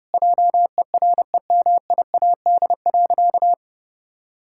Morse code, heard as a keyed tone, about 30 wpm, 700 Hz. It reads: JEREMIAD.
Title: Pangram Challenge v2 - 30wpm